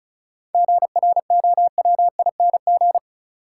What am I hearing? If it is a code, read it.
Morse code, heard as a keyed tone, about 36 wpm, 700 Hz. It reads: GROWING
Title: Sets of 5 Words Encore - New Word Drill - 35wpm